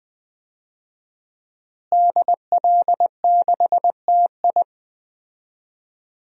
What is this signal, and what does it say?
Morse code, heard as a keyed tone, about 20 wpm, 700 Hz. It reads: DL6TI
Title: DXCC-100 Call Signs - 5 Characters - Repeated 2x - 20wpm